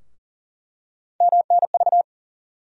Morse code, heard as a keyed tone, about 40 wpm, 700 Hz. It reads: MNV
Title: USA Call Sign Postfixes - 3 Letters 40wpm